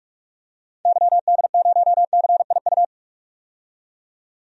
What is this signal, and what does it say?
Morse code, heard as a keyed tone, about 44 wpm, 690 Hz. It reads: YD0CIU
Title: DXCC-100 Call Signs - 6 Characters - 45wpm